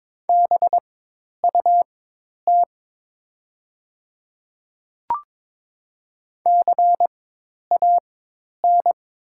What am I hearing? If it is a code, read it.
Morse code, heard as a keyed tone, about 22 wpm, spaced as for 10 wpm, 700 Hz. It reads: BUT CAN
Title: 4x Character Spacing - Top 100 Words - 22wpm